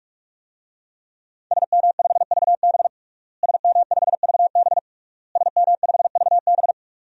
Morse code, heard as a keyed tone, about 46 wpm, 700 Hz. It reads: SM5VB SM5VB SM5VB
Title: DXCC-100 Call Signs - 5 Characters - Repeated 3x - 45wpm